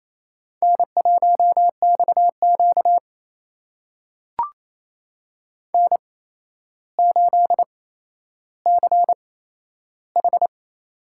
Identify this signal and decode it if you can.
Morse code, heard as a keyed tone, about 28 wpm, spaced as for 12 wpm, 700 Hz. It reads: N1XQ N8CH
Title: USA Call Signs - 4 Characters - 8x Character Spacing - 28wpm